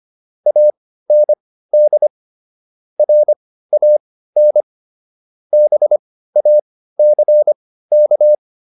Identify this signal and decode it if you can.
Morse code, heard as a keyed tone, about 25 wpm, spaced as for 15 wpm, 600 Hz. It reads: AND RAN BACK
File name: SS119 15